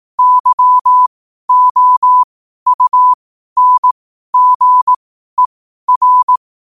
Morse code, heard as a keyed tone, about 18 wpm, spaced as for 12 wpm, 1000 Hz. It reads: YOUNGER
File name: qotdfiles / 12wpm